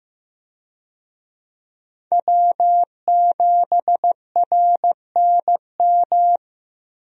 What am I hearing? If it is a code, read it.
Morse code, heard as a keyed tone, about 15 wpm, 700 Hz. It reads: W7RNM
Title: USA Call Signs - 5 Characters - 15wpm